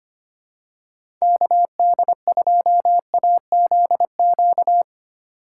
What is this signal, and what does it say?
Morse code, heard as a keyed tone, about 25 wpm, 700 Hz. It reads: KD2AZQ
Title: USA Call Signs - 6 Characters - 4x Character Spacing - 25wpm